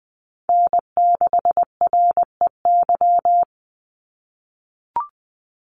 Morse code, heard as a keyed tone, about 20 wpm, 700 Hz. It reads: N6REY
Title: USA Call Signs 20wpm